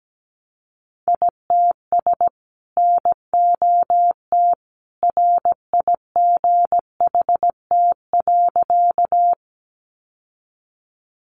Morse code, heard as a keyed tone, about 17 wpm, 700 Hz. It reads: ITS NOT RIGHT.
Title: Sentences from Top 600 Words 17wpm